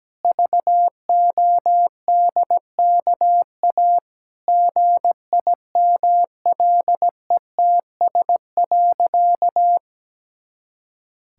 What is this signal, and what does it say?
Morse code, heard as a keyed tone, about 17 wpm, 700 Hz. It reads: VODKA GIMLETS.
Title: Pangram Challenge v2 - 17wpm